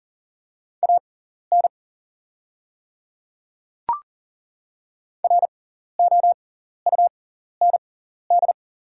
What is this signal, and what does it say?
Morse code, heard as a keyed tone, about 40 wpm, spaced as for 14 wpm, 690 Hz. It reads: AN ROUND
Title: 6x Character Spacing - Top 300 Words - 40wpm